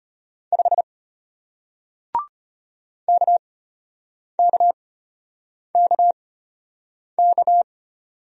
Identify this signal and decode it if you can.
Morse code, heard as a keyed tone, about 33 wpm, spaced as for 8 wpm, 700 Hz. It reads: 5 KKKK